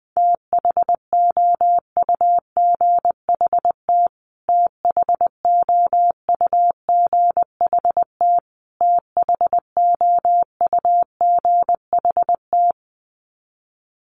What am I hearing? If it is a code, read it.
Morse code, heard as a keyed tone, about 20 wpm, 700 Hz. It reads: THOUGHT THOUGHT THOUGHT
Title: Top 500 Words Repeated 20wpm